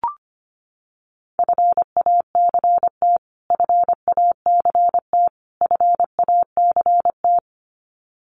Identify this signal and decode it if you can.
Morse code, heard as a keyed tone, about 25 wpm, 700 Hz. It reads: FACT FACT FACT
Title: Top 500 Words Repeated 25wpm